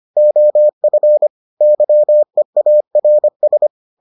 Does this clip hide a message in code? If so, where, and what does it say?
Yes: Morse code from 0.2 to 3.7 s, heard as a keyed tone, about 25 wpm, 600 Hz. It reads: OF YEARS